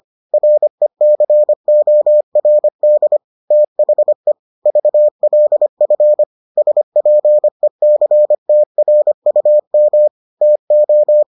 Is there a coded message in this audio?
Yes: Morse code from 0.3 to 11.2 s, heard as a keyed tone, about 25 wpm, 600 Hz. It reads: RECORD THE VLF SPECTRUM TO